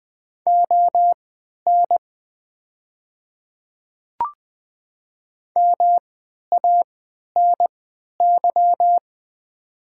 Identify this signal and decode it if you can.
Morse code, heard as a keyed tone, about 20 wpm, spaced as for 11 wpm, 700 Hz. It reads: ON MANY
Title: 3x Character Spacing - Top 100 Words - 20wpm